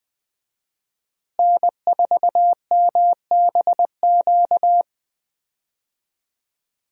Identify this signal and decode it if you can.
Morse code, heard as a keyed tone, about 20 wpm, 700 Hz. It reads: N4MBQ
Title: USA Call Signs - 5 Characters - Repeated 3x - 20wpm